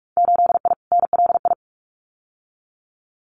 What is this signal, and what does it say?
Morse code, heard as a keyed tone, about 46 wpm, 700 Hz. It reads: 8I NLI
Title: Winter Field Day Exchanges Encore 45wpm